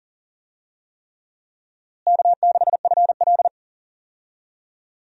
Morse code, heard as a keyed tone, about 40 wpm, 700 Hz. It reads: K6FL
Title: USA Call Signs - 4 Characters - Repeated 3x - 40wpm